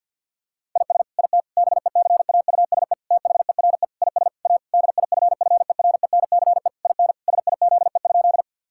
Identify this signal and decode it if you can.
Morse code, heard as a keyed tone, about 50 wpm, 700 Hz. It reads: IS IT BECAUSE THERE IS A DIFFERENCE IN SIZE?
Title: Sentences from Top 1000 Words - Review Part 5 - 50wpm